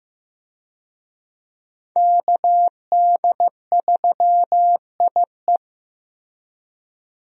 Morse code, heard as a keyed tone, about 15 wpm, 700 Hz. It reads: KD3IE
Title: USA Call Signs - 5 Characters - Repeated 3x - 15wpm